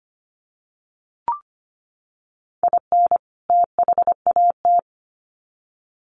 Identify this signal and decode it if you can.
Morse code, heard as a keyed tone, about 25 wpm, 700 Hz. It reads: IN THAT
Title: Sets of 2 Words 25wpm